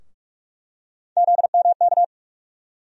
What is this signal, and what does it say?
Morse code, heard as a keyed tone, about 46 wpm, 700 Hz. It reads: ZMK
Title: USA Call Sign Postfixes - 3 Letters 45wpm